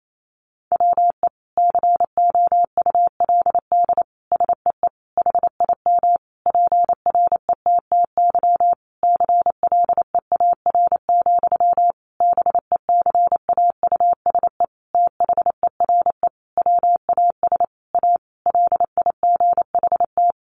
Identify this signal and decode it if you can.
Morse code, heard as a keyed tone, about 28 wpm, 700 Hz. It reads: WE COULD SEE HIM PRETTY CLEAR, BECAUSE THERE WAS A LIGHT
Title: Sentences from Top 1000 Words 28wpm